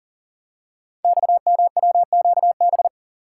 Morse code, heard as a keyed tone, about 43 wpm, 700 Hz. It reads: XMWQB
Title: Code Groups - 5 Letter:Numbers - 40wpm